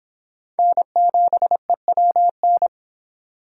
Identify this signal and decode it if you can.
Morse code, heard as a keyed tone, about 24 wpm, 700 Hz. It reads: N7EWN